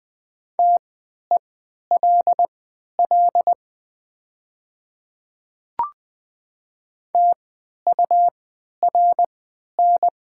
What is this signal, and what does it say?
Morse code, heard as a keyed tone, about 20 wpm, spaced as for 11 wpm, 700 Hz. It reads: TELL TURN
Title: 3x Character Spacing - Top 200 Words - 20wpm